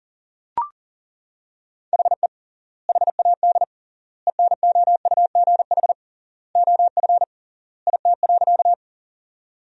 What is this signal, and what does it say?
Morse code, heard as a keyed tone, about 37 wpm, 700 Hz. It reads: HE HAD ENOUGH OF IT.
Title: 3x Word-Spacing - Sentences from Top 400 Words - 40wpm